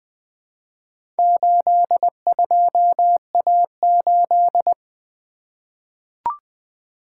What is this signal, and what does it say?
Morse code, heard as a keyed tone, about 20 wpm, 700 Hz. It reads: 82A8
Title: Four Letter-Number Combinations - 20wpm